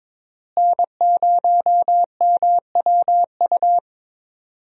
Morse code, heard as a keyed tone, about 22 wpm, 700 Hz. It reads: N0MWU
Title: USA Call Signs - 5 Characters - Repeated 3x - 22wpm